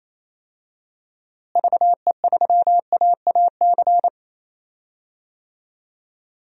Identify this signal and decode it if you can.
Morse code, heard as a keyed tone, about 28 wpm, 700 Hz. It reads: VE3AAC
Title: DXCC-100 Call Signs - 6 Characters - Repeated 3x - 28wpm